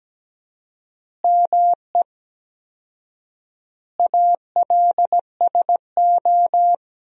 Morse code, heard as a keyed tone, about 17 wpm, 700 Hz. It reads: ME ALSO
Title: 4x Word-Spacing - Sentences from Top 400 Words - 17wpm